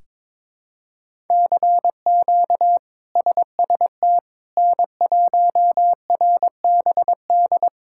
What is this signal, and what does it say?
Morse code, heard as a keyed tone, about 22 wpm, 700 Hz. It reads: CQ SST N1RBD
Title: K1USN SST QSO Practice - 22wpm